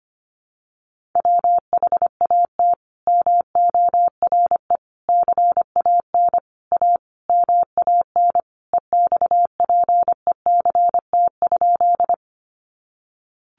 Morse code, heard as a keyed tone, about 25 wpm, 700 Hz. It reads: WHAT MORE CAN A MAN EXPECT?
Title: Sentences from Top 1100 Words 25wpm